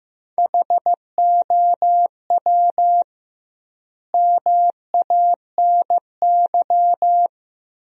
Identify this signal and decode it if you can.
Morse code, heard as a keyed tone, about 16 wpm, 700 Hz. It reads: HOW MANY
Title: 2x Word-Spacing - Sentences from Top 400 Words - 15wpm